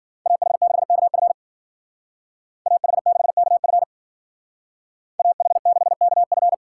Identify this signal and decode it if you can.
Morse code, heard as a keyed tone, about 52 wpm, 700 Hz. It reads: AH6KF AH6KF AH6KF